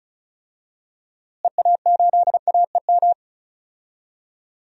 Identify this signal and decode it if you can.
Morse code, heard as a keyed tone, about 36 wpm, 700 Hz. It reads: EA8AEM
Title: DXCC-100 Call Signs - 6 Characters - Repeated 3x - 35wpm